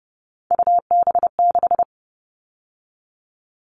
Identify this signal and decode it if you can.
Morse code, heard as a keyed tone, about 30 wpm, 700 Hz. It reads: UB6
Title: All Cut Numbers - 3-Digit 30wpm